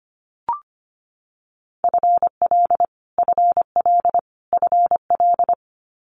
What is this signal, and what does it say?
Morse code, heard as a keyed tone, about 25 wpm, 700 Hz. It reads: FL FL FL
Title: Multi-letter Phonemes Repeated 3x 25wpm